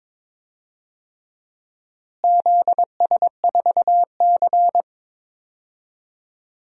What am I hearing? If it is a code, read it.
Morse code, heard as a keyed tone, about 22 wpm, 700 Hz. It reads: ZS4C
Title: DXCC-100 Call Signs - 4 Characters - Repeated 3x - 22wpm